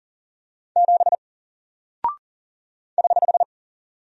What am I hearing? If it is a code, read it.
Morse code, heard as a keyed tone, about 40 wpm, 700 Hz. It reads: 7 <HH>
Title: Single Character (Letter:Number:Punctuation:Prosign) Alt 40wpm